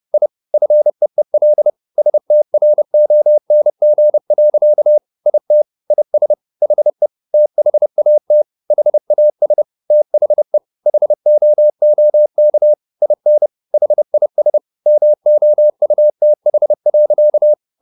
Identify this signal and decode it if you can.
Morse code, heard as a keyed tone, about 30 wpm, 600 Hz. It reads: I FEEL STRONG. IT IS HE THAT HAS THE HOOK IN HIS MOUTH.
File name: the old man and the sea / the old man and the sea 1 30wpm